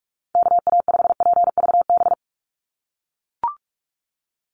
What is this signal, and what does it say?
Morse code, heard as a keyed tone, about 44 wpm, 700 Hz. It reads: KA5PVB
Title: NA SOTA Chaser Callsigns 45wpm